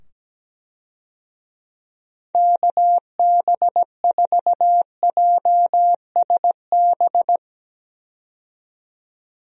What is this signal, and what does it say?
Morse code, heard as a keyed tone, about 17 wpm, 700 Hz. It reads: KB4JSB
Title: USA Call Signs - 6 Characters - 17wpm